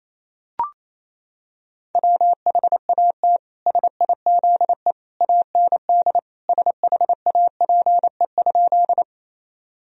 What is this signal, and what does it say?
Morse code, heard as a keyed tone, about 28 wpm, 700 Hz. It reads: WHAT SIZE AND SHAPE?
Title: Sentences from Top 1000 Words - Review Part 3 - 28wpm